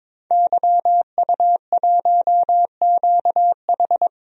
Morse code, heard as a keyed tone, about 22 wpm, 700 Hz. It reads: YU1QH
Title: DXCC-100 Call Signs - 5 Characters - 22wpm